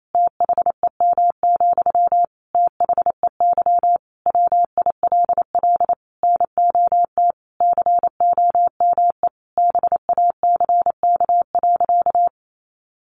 Morse code, heard as a keyed tone, about 28 wpm, 700 Hz. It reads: THEM, THEY WILL NOT COME BACK.
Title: Sentences from Top 1300 Words 28wpm